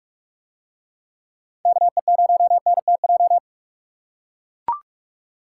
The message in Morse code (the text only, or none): KE0NTJ